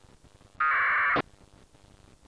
Packet radio (AX.25, as in APRS).
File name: satellite / raft / raft psk31 pe1itr 28jan2007 0622UTC